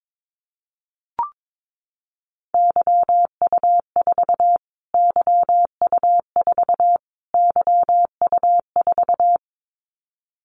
Morse code, heard as a keyed tone, about 22 wpm, 700 Hz. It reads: YU4 YU4 YU4
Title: Callsign Prefixes for Top 25 DXCC - Repeated 3x - 22wpm